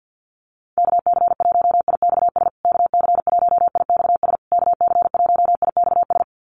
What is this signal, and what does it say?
Morse code, heard as a keyed tone, about 50 wpm, 700 Hz. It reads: KC1IXS KC1IXS KC1IXS
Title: USA Call Signs Repeated 3x 50wpm